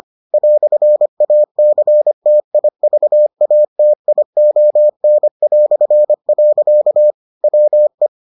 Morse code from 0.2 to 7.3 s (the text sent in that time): "ACTIVATION".